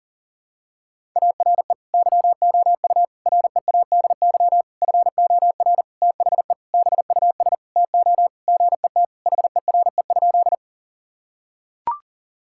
Morse code, heard as a keyed tone, about 37 wpm, 700 Hz. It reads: ARE YOU READY FOR THE BUS TO GET HERE?
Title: Sentences from Top 1000 Words - Review Part 6 - 40wpm